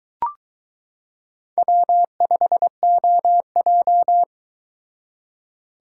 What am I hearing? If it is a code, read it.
Morse code, heard as a keyed tone, about 22 wpm, 700 Hz. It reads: W5OJ